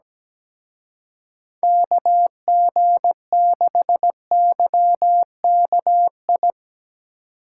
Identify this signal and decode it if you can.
Morse code, heard as a keyed tone, about 18 wpm, 700 Hz. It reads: KG6YKI
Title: Speed Racing - USA 2x3 Call Signs 17wpm